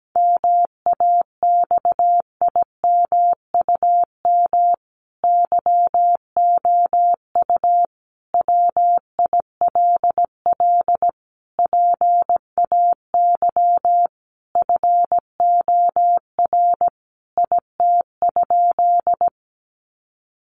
Morse code, heard as a keyed tone, about 17 wpm, 700 Hz. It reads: MAXIMUM YOU WILL PAY FOR IT?
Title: Sentences from Top 1300 Words 17wpm